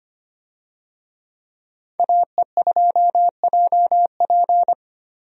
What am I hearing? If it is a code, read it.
Morse code, heard as a keyed tone, about 25 wpm, 700 Hz. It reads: AE2JP
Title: USA Call Signs - 5 Characters - Repeated 3x - 25wpm